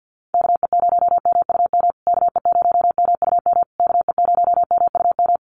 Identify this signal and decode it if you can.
Morse code, heard as a keyed tone, about 50 wpm, 700 Hz. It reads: KE0MUM KE0MUM KE0MUM
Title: USA Call Signs Repeated 3x 50wpm